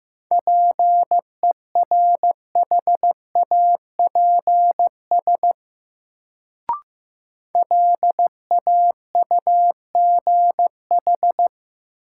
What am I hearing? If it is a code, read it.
Morse code, heard as a keyed tone, about 15 wpm, 700 Hz. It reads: PERHAPS LAUGH
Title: Sentences from Top 500 Words - New Word Drill - 15wpm